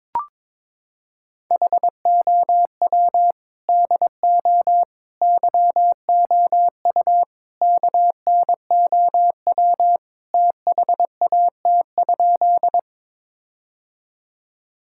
Morse code, heard as a keyed tone, about 21 wpm, 700 Hz. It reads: HOW DO YOU KNOW THAT?
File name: practice / 22wpm / Sentences with 4-Word Phrases - 22wpm